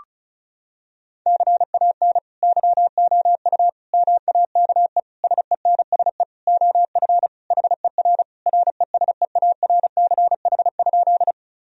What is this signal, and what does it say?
Morse code, heard as a keyed tone, about 36 wpm, 700 Hz. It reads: CAN YOU MAKE SENSE OF HER RESEARCH?
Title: Sentences from Top 1000 Words - Review Part 4 - 35wpm